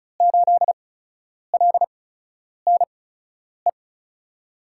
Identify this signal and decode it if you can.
Morse code, heard as a keyed tone, about 36 wpm, spaced as for 10 wpm, 700 Hz. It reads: 8LNE
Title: USA Call Signs - 5 Characters - 8x Character Spacing - 35wpm